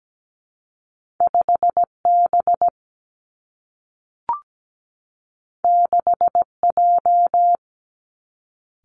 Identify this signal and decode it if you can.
Morse code, heard as a keyed tone, about 17 wpm, 700 Hz. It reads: 5B 6J
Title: Two Challenging Letter-Number Combinations 17wpm